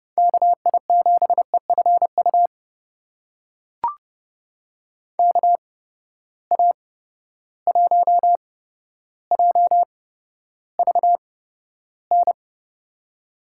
Morse code, heard as a keyed tone, about 32 wpm, spaced as for 13 wpm, 700 Hz. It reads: KI7EFU KA1JVN
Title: USA Call Signs - 6 Characters - 8x Character Spacing - 30wpm